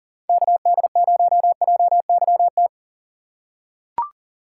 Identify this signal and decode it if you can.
Morse code, heard as a keyed tone, about 37 wpm, 700 Hz. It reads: KD0JYT
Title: USA Call Signs - 6 Characters - 6x Character Spacing - 40wpm